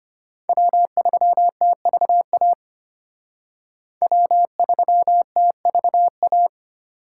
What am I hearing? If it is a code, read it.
Morse code, heard as a keyed tone, about 32 wpm, 700 Hz. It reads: W3TVA W3TVA